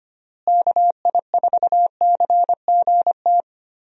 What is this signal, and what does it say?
Morse code, heard as a keyed tone, about 25 wpm, 700 Hz. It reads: KI4CGT
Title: USA Call Signs - 6 Characters - Repeated 2x - 25wpm